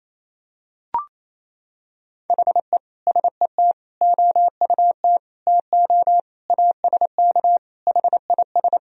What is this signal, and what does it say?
Morse code, heard as a keyed tone, about 28 wpm, 700 Hz. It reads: HE SET OUT TO ASK HIS
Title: Sentences from Top 1000 Words - Review Part 1 - 28wpm